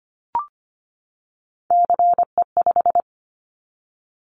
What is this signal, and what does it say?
Morse code, heard as a keyed tone, about 25 wpm, 700 Hz. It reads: CE5
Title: Callsign Prefixes for Top 100 DXCC - 25wpm